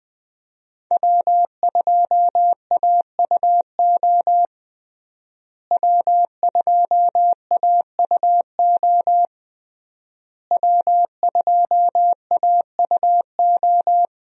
Morse code, heard as a keyed tone, about 20 wpm, 700 Hz. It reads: W2AUO W2AUO W2AUO
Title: USA Call Signs - 5 Characters - Repeated 3x - 20wpm